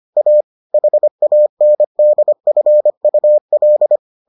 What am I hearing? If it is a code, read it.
Morse code, heard as a keyed tone, about 25 wpm, 600 Hz. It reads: A HANDFUL